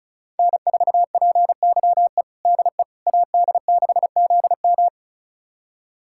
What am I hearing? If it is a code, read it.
Morse code, heard as a keyed tone, about 36 wpm, 700 Hz. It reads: N4PYE DE AD6ZM
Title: USA Callsign DE Callsign - 5 Characters - 35wpm